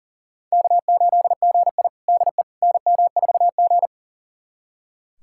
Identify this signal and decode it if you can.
Morse code, heard as a keyed tone, about 40 wpm, 700 Hz. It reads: K8GI DE NM4G
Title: USA Callsign DE Callsign - 4 Characters - 40wpm